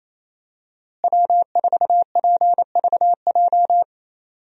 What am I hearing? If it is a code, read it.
Morse code, heard as a keyed tone, about 28 wpm, 700 Hz. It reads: W4PVJ
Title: USA Call Signs - 5 Characters - Repeated 2x - 28wpm